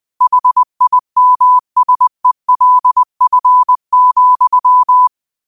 Morse code, heard as a keyed tone, about 20 wpm, 1000 Hz. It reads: HIMSELF,
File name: qotdfiles / 20wpm